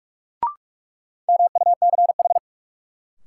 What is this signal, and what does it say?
Morse code, heard as a keyed tone, about 46 wpm, 700 Hz. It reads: MUCH
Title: Sentences from Top 200 Words - New Word Drill - 45wpm